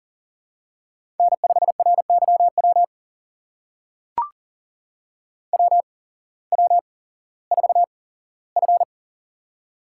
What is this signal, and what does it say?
Morse code, heard as a keyed tone, about 38 wpm, spaced as for 18 wpm, 700 Hz. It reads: N5RYW WW4F
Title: USA Call Signs - 8x Character Spacing - 40wpm